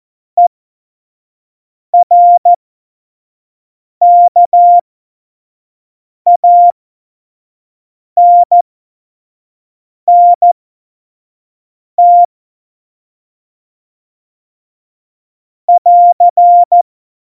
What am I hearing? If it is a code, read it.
Morse code, heard as a keyed tone, about 14 wpm, spaced as for 5 wpm, 700 Hz. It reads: ERKANNT <AR>
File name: Morsekurs / Witze / Witz 23. Januar 2018, 25 F70